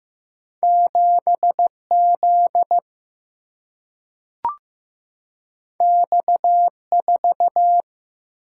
Morse code, heard as a keyed tone, about 15 wpm, 700 Hz. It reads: 7 Z X 4
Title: Single-Letter Single-Number 15wpm